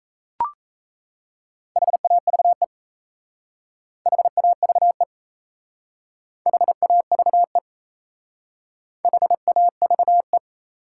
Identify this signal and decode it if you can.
Morse code, heard as a keyed tone, about 34 wpm, 700 Hz. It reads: HAVE HAVE HAVE HAVE
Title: Speed-Racing Top 180 QSO Elements 28wpm